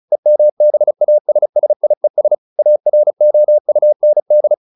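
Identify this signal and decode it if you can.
Morse code, heard as a keyed tone, about 34 wpm, 600 Hz. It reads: EMBASSIES AROUND
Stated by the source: Morse practice at 35 wpm from CW med Gnister, sending Ken Follett's novel Never in English.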